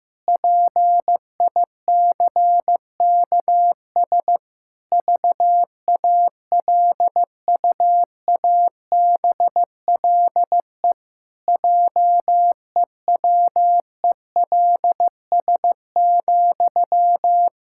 Morse code, heard as a keyed tone, about 15 wpm, 700 Hz. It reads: PICKS VALUABLE JEWELS,
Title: Pangram Challenge v2 - 15wpm